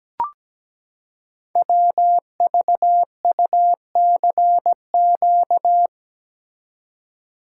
Morse code, heard as a keyed tone, about 17 wpm, 700 Hz. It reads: WVUCQ
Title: Code Groups - 5 Letters - 17wpm